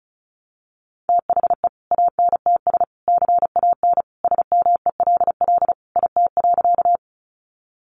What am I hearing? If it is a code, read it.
Morse code, heard as a keyed tone, about 34 wpm, 700 Hz. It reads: THE ANTS CAN SMELL IT.
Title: Easy Phrases 35wpm